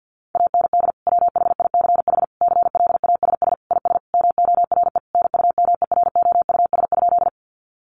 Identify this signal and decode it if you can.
Morse code, heard as a keyed tone, about 50 wpm, 700 Hz. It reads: AND WHICH CLASS IS MORE NUMEROUS?
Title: Sentences from Top 1400 Words 50wpm